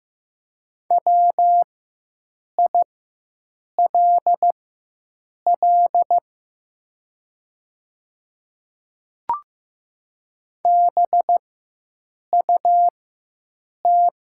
Morse code, heard as a keyed tone, about 15 wpm, spaced as for 7 wpm, 700 Hz. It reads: WILL BUT
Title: 4x Character Spacing - Top 100 Words - 15wpm